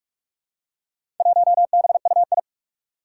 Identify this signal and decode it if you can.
Morse code, heard as a keyed tone, about 44 wpm, 700 Hz. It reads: 1BUI
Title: Four Challenging Letter-Number Combinations 45wpm